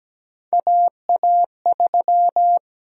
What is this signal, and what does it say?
Morse code, heard as a keyed tone, about 17 wpm, 700 Hz. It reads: AA3
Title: USA Call Sign Prefixes - Repeated 2x - 17wpm